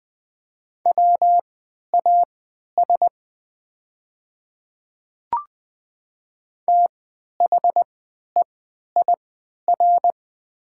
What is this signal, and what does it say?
Morse code, heard as a keyed tone, about 20 wpm, spaced as for 11 wpm, 700 Hz. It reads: WAS THEIR
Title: 3x Character Spacing - Top 100 Words - 20wpm